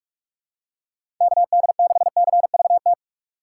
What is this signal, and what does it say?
Morse code, heard as a keyed tone, about 45 wpm, 700 Hz. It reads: KD6CVT
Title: USA Call Signs - 6 Characters - 4x Character Spacing - 45wpm